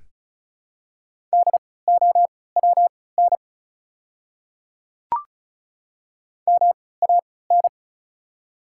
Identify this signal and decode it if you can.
Morse code, heard as a keyed tone, about 36 wpm, spaced as for 20 wpm, 700 Hz. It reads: DOWN MAN
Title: 3x Character Spacing - Top 300 Words - 35wpm